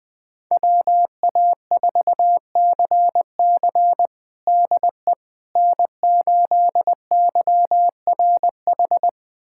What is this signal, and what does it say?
Morse code, heard as a keyed tone, about 20 wpm, 700 Hz. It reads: WA4CC DE N8YRH
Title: USA Callsign DE Callsign - 5 Characters - 20wpm